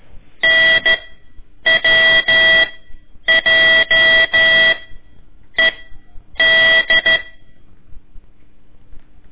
Morse code, heard as a keyed tone, about 13 wpm, 1950 Hz. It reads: N W J E D
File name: ik1weg2 / CORSI CW / lez4-1